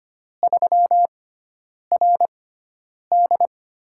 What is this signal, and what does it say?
Morse code, heard as a keyed tone, about 25 wpm, spaced as for 9 wpm, 700 Hz. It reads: 3RD
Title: USA Call Signs - 4 Characters - 6x Character Spacing - 25wpm